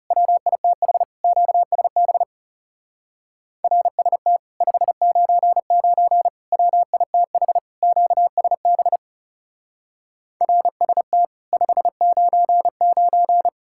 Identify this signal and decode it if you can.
Morse code, heard as a keyed tone, about 38 wpm, 700 Hz. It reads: WITH QSB RST 599 WITH QSB RST 599